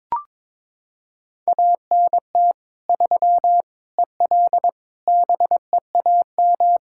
Morse code, heard as a keyed tone, about 22 wpm, 700 Hz. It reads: ANT 3 EL BEAM
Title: Practice QSO - Exchange 2 - Info - 22wpm